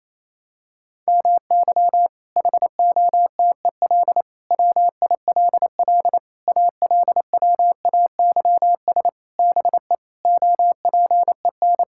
Morse code, heard as a keyed tone, about 28 wpm, 700 Hz. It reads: MY HOTEL WILL ALWAYS BE OPEN
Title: Sentences from Top 1600 Words 28wpm